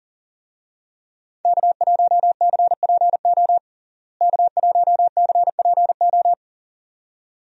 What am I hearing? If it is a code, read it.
Morse code, heard as a keyed tone, about 40 wpm, 700 Hz. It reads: K1CPO K1CPO
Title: USA Call Signs - 5 Characters - Repeated 2x - 40wpm